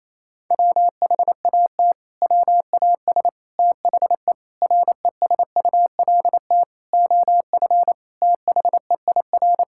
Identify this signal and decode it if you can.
Morse code, heard as a keyed tone, about 28 wpm, 700 Hz. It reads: WHAT WAS THE RESULT OF THEIR
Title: Sentences from Top 1000 Words - Review Part 5 - 28wpm